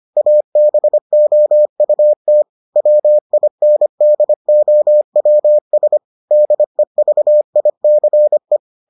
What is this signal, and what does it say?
Morse code, heard as a keyed tone, about 25 wpm, 600 Hz. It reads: ABOUT WINDOWS DEVICE